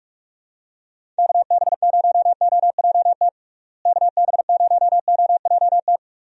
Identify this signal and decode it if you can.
Morse code, heard as a keyed tone, about 42 wpm, 700 Hz. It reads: KB0OJT KB0OJT
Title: USA Call Signs - 6 Characters - Repeated 2x - 45wpm